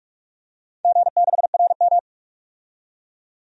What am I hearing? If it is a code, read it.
Morse code, heard as a keyed tone, about 44 wpm, 690 Hz. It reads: G6RM
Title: DXCC-100 Call Signs - 4 Characters - 45wpm